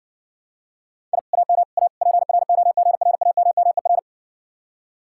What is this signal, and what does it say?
Morse code, heard as a keyed tone, about 68 wpm, 700 Hz. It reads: I AM A PROGRAMMER